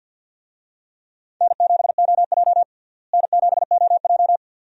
Morse code, heard as a keyed tone, about 46 wpm, 700 Hz. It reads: N7OJ N7OJ
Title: USA Call Signs - 4 Characters - Repeated 2x - 50wpm